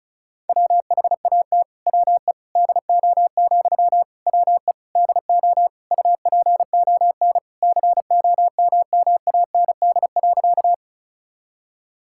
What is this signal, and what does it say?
Morse code, heard as a keyed tone, about 36 wpm, 700 Hz. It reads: WHAT WE DO, WE DO UPON COMMAND.
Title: Sentences from Top 1600 Words 35wpm